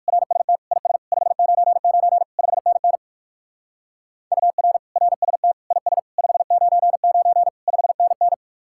Morse code, heard as a keyed tone, about 53 wpm, 700 Hz. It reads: RST IS 599 5NN UR RST IS 599 5NN
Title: Speed-Racing Common QSO Phrases - 45wpm